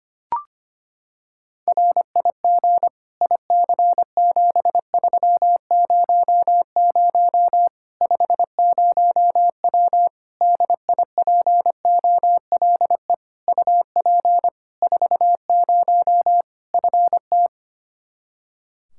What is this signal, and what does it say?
Morse code, heard as a keyed tone, about 25 wpm, 700 Hz. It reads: RIG IC7300 50W DIPOLE UP 40 FT
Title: Practice QSO - Exchange 2 - Info - 25wpm